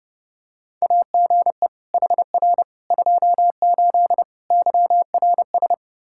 Morse code, heard as a keyed tone, about 32 wpm, 700 Hz. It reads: AGE HR 28 YRS
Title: Practice QSO - Exchange 3 - More Info - 30wpm